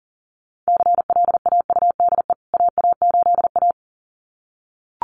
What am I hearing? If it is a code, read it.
Morse code, heard as a keyed tone, about 40 wpm, 700 Hz. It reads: CLAUDE AA8A
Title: CWT Exchange 40wpm